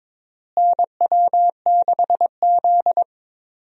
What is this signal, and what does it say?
Morse code, heard as a keyed tone, about 22 wpm, 700 Hz. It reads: NW6Z
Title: USA Call Signs - 4 Characters - Repeated 3x - 22wpm